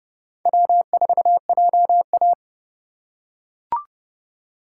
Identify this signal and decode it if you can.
Morse code, heard as a keyed tone, about 31 wpm, 700 Hz. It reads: W4JA
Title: USA Call Signs - 4 Characters - 2x Character Spacing - 30wpm